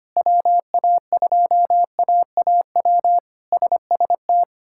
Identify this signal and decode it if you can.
Morse code, heard as a keyed tone, about 25 wpm, 700 Hz. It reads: WA2AAW SST
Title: K1USN SST QSO Practice - 25wpm